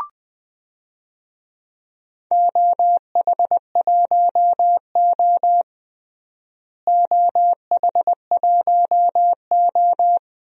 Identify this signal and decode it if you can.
Morse code, heard as a keyed tone, about 20 wpm, 700 Hz. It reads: OH1O OH1O
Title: DXCC-100 Call Signs - 4 Characters - Repeated 3x - 20wpm